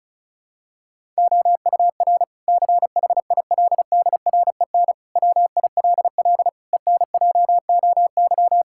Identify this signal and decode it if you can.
Morse code, heard as a keyed tone, about 36 wpm, 700 Hz. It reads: OUR CHILDREN WILL ENJOY
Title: Sentences from Top 1000 Words - Review Part 5 - 35wpm